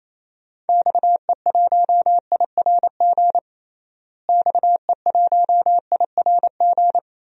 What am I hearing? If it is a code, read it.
Morse code, heard as a keyed tone, about 28 wpm, 700 Hz. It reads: XE1IRG XE1IRG
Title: DXCC-100 Call Signs - 6 Characters - Repeated 2x - 28wpm